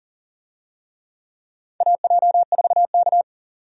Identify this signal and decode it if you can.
Morse code, heard as a keyed tone, about 40 wpm, 700 Hz. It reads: AJ4K
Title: USA Call Signs - 4 Characters - 40wpm